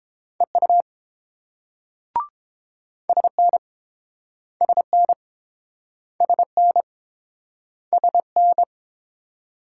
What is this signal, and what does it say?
Morse code, heard as a keyed tone, about 30 wpm, 700 Hz. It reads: EU SN SN SN SN